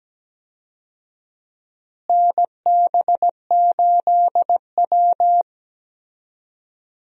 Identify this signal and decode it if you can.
Morse code, heard as a keyed tone, about 17 wpm, 700 Hz. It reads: NB8W
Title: USA Call Signs - 4 Characters - Repeated 2x - 17wpm